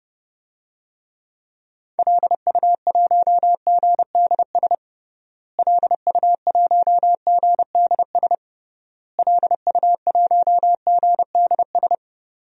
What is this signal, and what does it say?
Morse code, heard as a keyed tone, about 31 wpm, 700 Hz. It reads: LU1GDS LU1GDS LU1GDS
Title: DXCC-100 Call Signs - 6 Characters - Repeated 3x - 30wpm